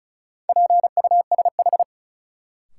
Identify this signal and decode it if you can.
Morse code, heard as a keyed tone, about 36 wpm, 700 Hz. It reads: PUSH
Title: Sentences from Top 1300 Words - New Word Drill - 35wpm